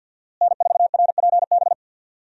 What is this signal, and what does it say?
Morse code, heard as a keyed tone, about 47 wpm, 700 Hz. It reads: N4RPB
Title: USA Call Signs - 5 Characters - Repeated 3x - 50wpm